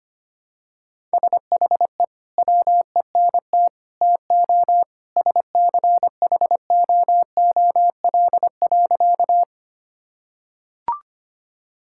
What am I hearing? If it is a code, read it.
Morse code, heard as a keyed tone, about 25 wpm, 700 Hz. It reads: SHE WENT TO SCHOOL.
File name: practice / 25wpm / Sentences with 2-Word Phrases - 25wpm